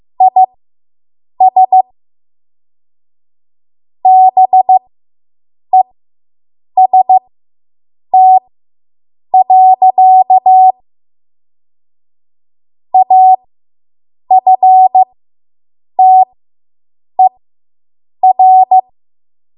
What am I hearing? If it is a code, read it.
Morse code, heard as a keyed tone, about 15 wpm, spaced as for 7 wpm, 750 Hz. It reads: IS BEST. AFTER